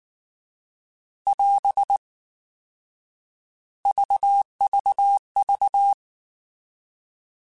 Morse code, heard as a keyed tone, about 19 wpm, 800 Hz. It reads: <AS> VVV